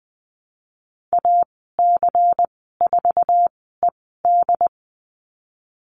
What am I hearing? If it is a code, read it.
Morse code, heard as a keyed tone, about 20 wpm, spaced as for 15 wpm, 700 Hz. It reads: AC4ED
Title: 147 Lesson 24g - Letter y - Call Signs 20wpm